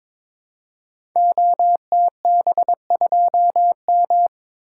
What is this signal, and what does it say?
Morse code, heard as a keyed tone, about 22 wpm, 700 Hz. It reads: OTB2M
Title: Code Groups - 5 Letter:Numbers - 22wpm